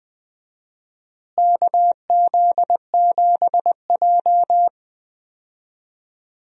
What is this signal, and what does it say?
Morse code, heard as a keyed tone, about 21 wpm, 700 Hz. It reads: KZ7J
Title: USA Call Signs - 4 Characters - Repeated 3x - 20wpm